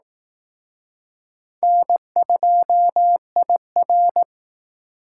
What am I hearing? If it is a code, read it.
Morse code, heard as a keyed tone, about 18 wpm, 700 Hz. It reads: N2IR
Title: Speed Racing - USA Call Signs 15wpm